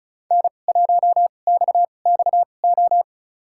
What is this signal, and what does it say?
Morse code, heard as a keyed tone, about 36 wpm, spaced as for 25 wpm, 700 Hz. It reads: N1XXO
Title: USA Call Signs - 5 Characters - 2x Character Spacing - 35wpm